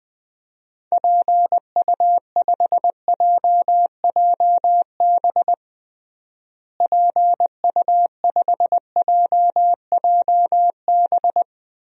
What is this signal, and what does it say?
Morse code, heard as a keyed tone, about 20 wpm, 700 Hz. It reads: PU5JJB PU5JJB
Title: DXCC-100 Call Signs - 6 Characters - Repeated 3x - 20wpm